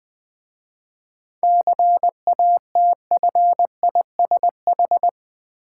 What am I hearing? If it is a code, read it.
Morse code, heard as a keyed tone, about 20 wpm, 690 Hz. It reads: CATFISH
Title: Difficult Words with Hints - 20wpm